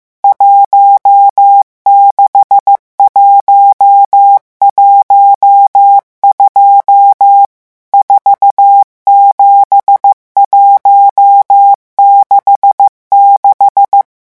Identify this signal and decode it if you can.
Morse code, heard as a keyed tone, about 15 wpm, 800 Hz. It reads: 16112 47166
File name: morse mp3 / revision11 chiffres vitesse 14